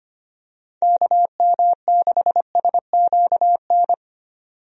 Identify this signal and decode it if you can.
Morse code, heard as a keyed tone, about 25 wpm, 700 Hz. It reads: KM6SQN
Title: USA Call Signs - 6 Characters - 25wpm